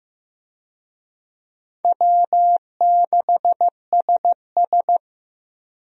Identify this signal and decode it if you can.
Morse code, heard as a keyed tone, about 15 wpm, 700 Hz. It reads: W6SS
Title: USA Call Signs - 4 Characters - Repeated 2x - 15wpm